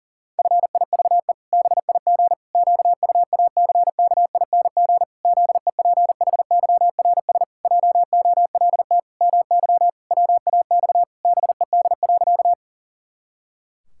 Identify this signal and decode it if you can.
Morse code, heard as a keyed tone, about 40 wpm, 700 Hz. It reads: FIVE BIG QUACKING ZEPHYRS JOLT MY WAX BED.
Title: Pangram Challenge v2 - 40wpm